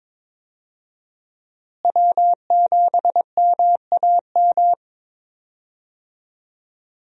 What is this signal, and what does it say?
Morse code, heard as a keyed tone, about 21 wpm, 700 Hz. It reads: W7MAM
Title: USA Call Signs - 5 Characters - Repeated 3x - 22wpm